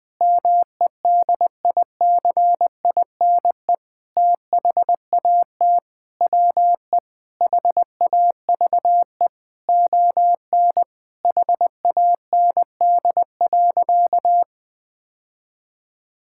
Morse code, heard as a keyed tone, about 19 wpm, 700 Hz. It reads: MEDICINE THAT WE HAVE ON HAND.
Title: Sentences from Top 1000 Words - Review Part 6 - 20wpm